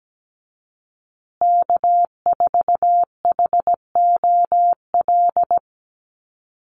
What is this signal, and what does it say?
Morse code, heard as a keyed tone, about 17 wpm, 700 Hz. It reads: K4HOL
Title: USA Call Signs Repeated 3x 17wpm